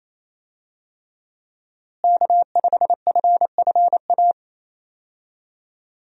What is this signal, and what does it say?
Morse code, heard as a keyed tone, about 28 wpm, 700 Hz. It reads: K5FFA
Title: USA Call Signs - 5 Characters - Repeated 3x - 28wpm